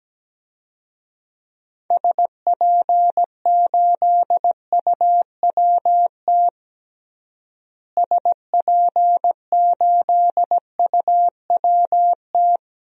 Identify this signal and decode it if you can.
Morse code, heard as a keyed tone, about 17 wpm, 700 Hz. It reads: SP8UWT SP8UWT
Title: DXCC-100 Call Signs - 6 Characters - Repeated 2x - 17wpm